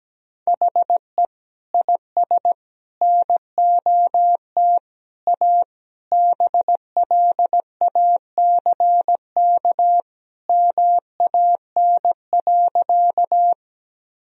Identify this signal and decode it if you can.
Morse code, heard as a keyed tone, about 17 wpm, 700 Hz. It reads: HE IS NOT A BLACK MAN.
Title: Sentences from Top 1000 Words - Review Part 2 - 17wpm